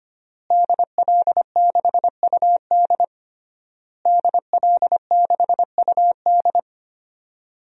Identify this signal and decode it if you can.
Morse code, heard as a keyed tone, about 25 wpm, 700 Hz. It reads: DL6UD DL6UD
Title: DXCC-100 Call Signs - 5 Characters - Repeated 3x - 25wpm